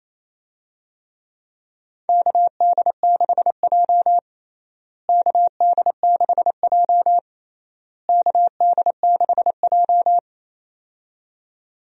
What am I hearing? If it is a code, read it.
Morse code, heard as a keyed tone, about 28 wpm, 700 Hz. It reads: KD6J KD6J KD6J
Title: USA Call Signs - 4 Characters - Repeated 3x - 28wpm